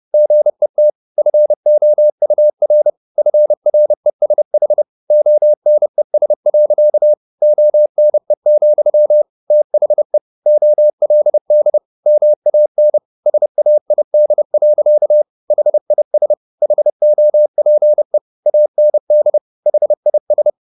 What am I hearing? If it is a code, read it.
Morse code, heard as a keyed tone, about 30 wpm, 600 Hz. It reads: GET FOUR FRESH ONES. ONE, THE OLD MAN SAID. HIS HOPE AND HIS
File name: the old man and the sea / the old man and the sea 1 30wpm